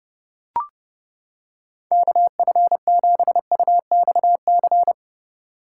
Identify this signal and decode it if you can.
Morse code, heard as a keyed tone, about 30 wpm, 700 Hz. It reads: KF7UXC
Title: K1USN SST QSO Practice - 30wpm